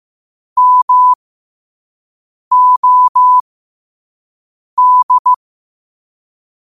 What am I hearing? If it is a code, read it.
Morse code, heard as a keyed tone, about 15 wpm, spaced as for 5 wpm, 1000 Hz. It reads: MOD
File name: qotdfiles / 5wpm